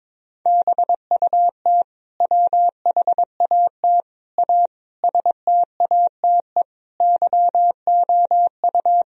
Morse code, heard as a keyed tone, about 22 wpm, 700 Hz. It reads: BUT WHAT A STATE YOU
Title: Sentences from Top 1000 Words - Review Part 2 - 22wpm